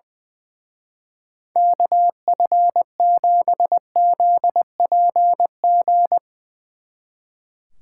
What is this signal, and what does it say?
Morse code, heard as a keyed tone, about 22 wpm, 700 Hz. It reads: KF7ZPG
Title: Speed Racing - USA 2x3 Call Signs 20wpm